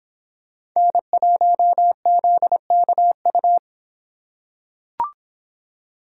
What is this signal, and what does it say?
Morse code, heard as a keyed tone, about 26 wpm, 700 Hz. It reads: N1ZKU